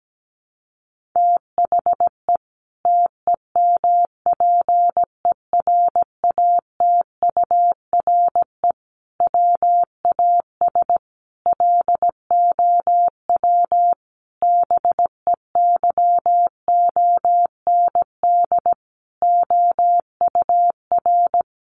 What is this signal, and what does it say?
Morse code, heard as a keyed tone, about 17 wpm, 700 Hz. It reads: THE TEMPERATURE WAS LOW BEYOND OUR
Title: Sentences from Top 1500 Words 17wpm